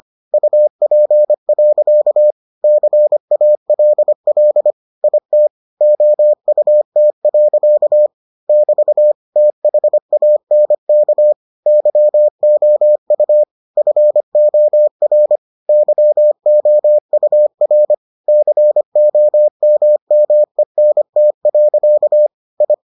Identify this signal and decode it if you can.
Morse code, heard as a keyed tone, about 25 wpm, 600 Hz. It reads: UP. CALL IT OUT. <BT> THANK YOU FOR YOUR COMMENT. I